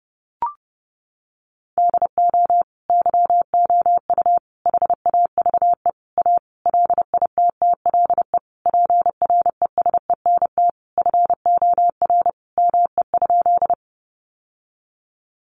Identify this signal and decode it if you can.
Morse code, heard as a keyed tone, about 30 wpm, 700 Hz. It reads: DO YOU HAVE A LITTLE PRESENT FOR ME?
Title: Sentences from Top 800 Words 30wpm